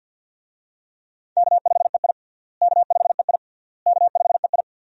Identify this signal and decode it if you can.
Morse code, heard as a keyed tone, about 50 wpm, 700 Hz. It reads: K5EI K5EI K5EI
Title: USA Call Signs - 4 Characters - Repeated 3x - 50wpm